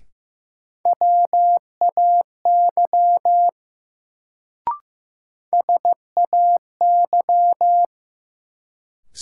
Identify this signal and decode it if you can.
Morse code, heard as a keyed tone, about 15 wpm, 700 Hz. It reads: WAY SAY
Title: Sets of 2 Words - New Word Drill - 15wpm